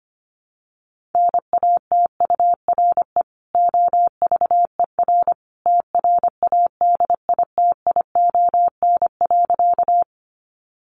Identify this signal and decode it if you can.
Morse code, heard as a keyed tone, about 25 wpm, 700 Hz. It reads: NATURE OVER TRADITION.
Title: Sentences from Top 1500 Words 25wpm